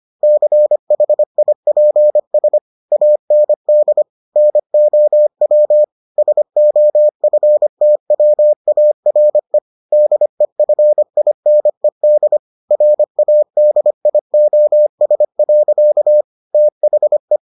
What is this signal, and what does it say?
Morse code, heard as a keyed tone, about 25 wpm, 600 Hz. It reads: CHIPS AND NOW SOFTWARE DEFINED RADIOS. THE